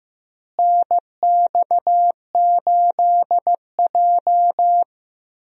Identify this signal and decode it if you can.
Morse code, heard as a keyed tone, about 15 wpm, 700 Hz. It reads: NX8J
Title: USA Call Signs - 4 Characters - Repeated 3x - 15wpm